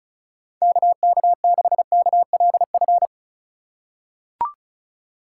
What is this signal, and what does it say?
Morse code, heard as a keyed tone, about 34 wpm, 700 Hz. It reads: KK6KLF
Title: USA Call Signs - 6 Characters - 2x Character Spacing - 35wpm